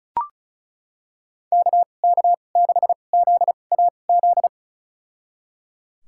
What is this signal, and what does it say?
Morse code, heard as a keyed tone, about 34 wpm, spaced as for 25 wpm, 700 Hz. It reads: KK6ZAZ
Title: USA Call Signs - 6 Characters - 2x Character Spacing - 35wpm